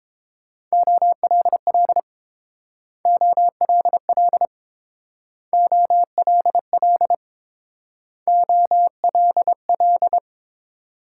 Morse code, heard as a keyed tone, about 29 wpm, 700 Hz. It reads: OLL OLL OLL OLL